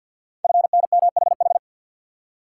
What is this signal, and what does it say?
Morse code, heard as a keyed tone, about 48 wpm, 700 Hz. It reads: FNMHH
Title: Code Groups - 5 Letters - 50wpm